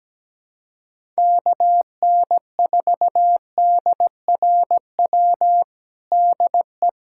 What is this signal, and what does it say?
Morse code, heard as a keyed tone, about 17 wpm, 700 Hz. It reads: KN4DRW DE
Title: USA Callsign DE Callsign 17wpm